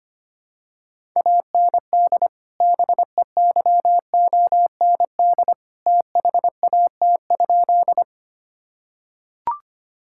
Morse code, heard as a keyed tone, about 25 wpm, 700 Hz. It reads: AND BEYOND THAT?
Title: Sentences from Top 1000 Words - Review Part 6 - 25wpm